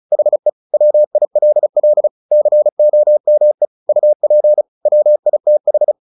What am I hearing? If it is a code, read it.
Morse code, heard as a keyed tone, about 34 wpm, 600 Hz. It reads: HE WILL COME UP WITH
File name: the old man and the sea / the old man and the sea 1 35wpm